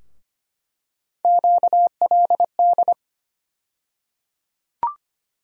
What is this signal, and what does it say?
Morse code, heard as a keyed tone, about 25 wpm, 700 Hz. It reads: QLD
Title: USA Call Sign Postfixes - 3 Letters 25wpm